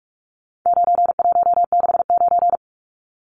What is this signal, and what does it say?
Morse code, heard as a keyed tone, about 44 wpm, 700 Hz. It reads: 9169
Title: Numbers - 4-Digit 45wpm